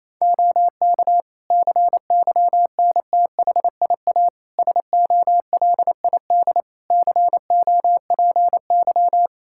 Morse code, heard as a keyed tone, about 27 wpm, 700 Hz. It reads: OK CYNTHIA SOLID COPY
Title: K1USN SST QSO Practice 28wpm